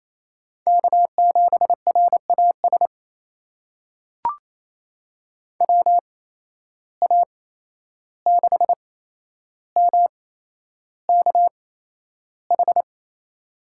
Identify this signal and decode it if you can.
Morse code, heard as a keyed tone, about 28 wpm, spaced as for 11 wpm, 700 Hz. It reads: K7RAS WA6MKH
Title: USA Call Signs - 8x Character Spacing - 28wpm